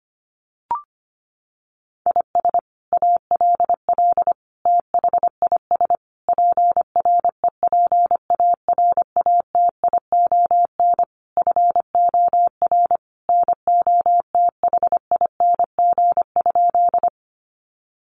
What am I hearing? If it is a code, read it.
Morse code, heard as a keyed tone, about 25 wpm, 700 Hz. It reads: IS ALL THIS PREPARATION FOR NOTHING?
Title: Sentences from Top 1400 Words 25wpm